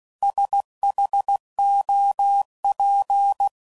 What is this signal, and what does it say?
Morse code, heard as a keyed tone, about 16 wpm, 800 Hz. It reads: SHOP
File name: CW Code Groups @15WPM / Weather Sets / Weather 4 24